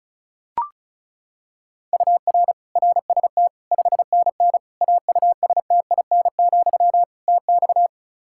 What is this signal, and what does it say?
Morse code, heard as a keyed tone, about 36 wpm, 700 Hz. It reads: UR RST 5NN AUSTIN, TX
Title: Practice QSO - Exchange 1 - Beginning - 35wpm